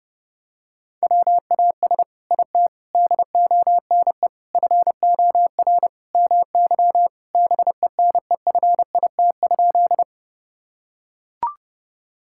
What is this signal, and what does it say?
Morse code, heard as a keyed tone, about 30 wpm, 700 Hz. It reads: WAS IT DONE FOR MY BENEFIT?
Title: Sentences from Top 1000 Words - Review Part 6 - 30wpm